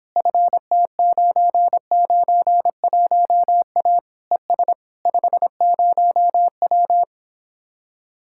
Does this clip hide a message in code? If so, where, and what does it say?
Yes: Morse code from 0.2 to 7.0 s, heard as a keyed tone, about 29 wpm, 700 Hz. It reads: FT991A ES 50W